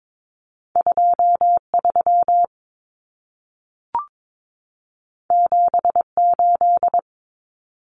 Morse code, heard as a keyed tone, about 22 wpm, 700 Hz. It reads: 23 78
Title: Common Cut Numbers - 2-Digit 22wpm